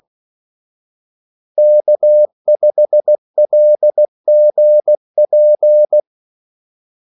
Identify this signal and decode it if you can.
Morse code, heard as a keyed tone, about 16 wpm, 600 Hz. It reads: K5LGP
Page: Kaldesignaler 16 wpm